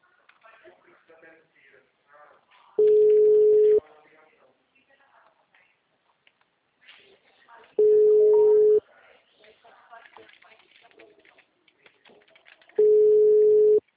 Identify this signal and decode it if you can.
Ringback tone (Europe, 425 Hz), with speech.
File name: out- -2004- - - .45530